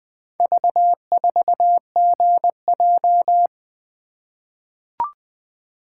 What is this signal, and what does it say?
Morse code, heard as a keyed tone, about 20 wpm, 700 Hz. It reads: V4GJ
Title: Four Letter-Number Combinations - 20wpm